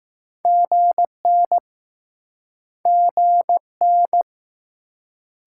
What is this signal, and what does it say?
Morse code, heard as a keyed tone, about 20 wpm, 700 Hz. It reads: GN GN